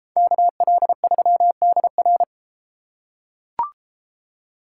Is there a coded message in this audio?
Yes: Morse code from 0.2 to 2.2 s, heard as a keyed tone, about 32 wpm, 700 Hz. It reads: KL3DR